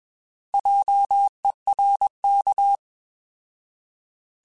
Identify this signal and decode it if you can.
Morse code, heard as a keyed tone, about 21 wpm, 800 Hz. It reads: JERK
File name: Alphabetical / Alphabetical mp3s / Js @20